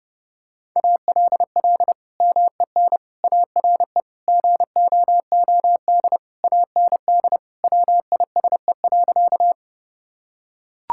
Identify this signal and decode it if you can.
Morse code, heard as a keyed tone, about 30 wpm, 700 Hz. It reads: ALL MEN ARE GOOD AND WISE.
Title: Sentences from Top 1600 Words 30wpm